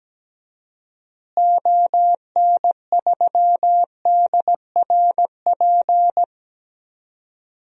Morse code, heard as a keyed tone, about 17 wpm, 700 Hz. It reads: ON3DRP
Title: DXCC-100 Call Signs - 6 Characters - Repeated 2x - 17wpm